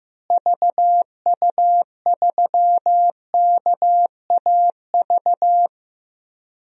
Morse code, heard as a keyed tone, about 15 wpm, 700 Hz. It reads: VU3KAV
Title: DXCC-100 Call Signs - 6 Characters - Repeated 3x - 15wpm